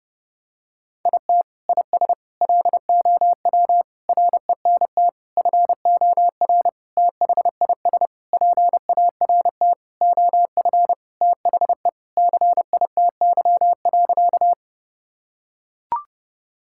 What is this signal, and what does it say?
Morse code, heard as a keyed tone, about 31 wpm, 700 Hz. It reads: IT IS LOW RENT FOR THIS PART OF THE CITY.
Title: Sentences from Top 1000 Words - Review Part 5 - 30wpm